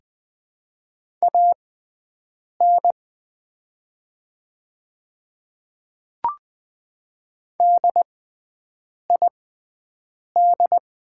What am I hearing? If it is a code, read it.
Morse code, heard as a keyed tone, about 20 wpm, spaced as for 7 wpm, 700 Hz. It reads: AN DID
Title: 6x Character Spacing - Top 500 Words - 20wpm